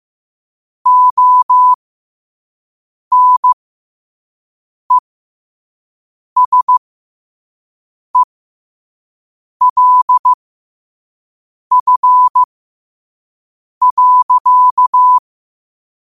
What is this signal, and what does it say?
Morse code, heard as a keyed tone, about 15 wpm, spaced as for 5 wpm, 1000 Hz. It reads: ONESELF.
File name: qotdfiles / 5wpm